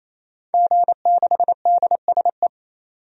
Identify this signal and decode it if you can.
Morse code, heard as a keyed tone, about 28 wpm, 700 Hz. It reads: G6DSE
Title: DXCC-100 Call Signs - 5 Characters - 28wpm